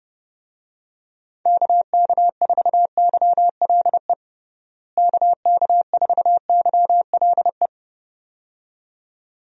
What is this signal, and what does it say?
Morse code, heard as a keyed tone, about 32 wpm, 700 Hz. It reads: KK4YLE KK4YLE
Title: USA Call Signs - 6 Characters - Repeated 2x - 30wpm